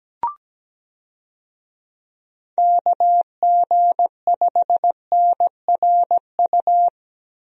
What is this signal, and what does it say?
Morse code, heard as a keyed tone, about 17 wpm, 700 Hz. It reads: KG5NRU
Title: USA Call Signs - 6 Characters - Repeated 3x - 17wpm